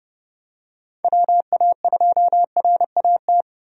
Morse code, heard as a keyed tone, about 30 wpm, 700 Hz. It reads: WA2RAT
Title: USA Call Signs - 6 Characters - 30wpm